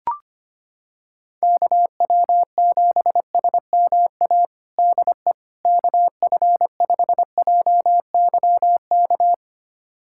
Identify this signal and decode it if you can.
Morse code, heard as a keyed tone, about 25 wpm, 700 Hz. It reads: KW7SMA DE KF5JYK
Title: USA Callsign DE Callsign 25wpm